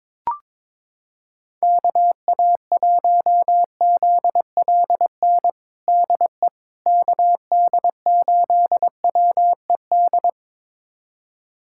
Morse code, heard as a keyed tone, about 22 wpm, 700 Hz. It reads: KA1ZLN DE KD8WED
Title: USA Callsign DE Callsign - 6 Characters - 22wpm